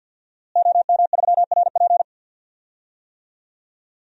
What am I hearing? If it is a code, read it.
Morse code, heard as a keyed tone, about 51 wpm, 700 Hz. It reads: OM3RP
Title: DXCC-100 Call Signs - 5 Characters - Repeated 3x - 50wpm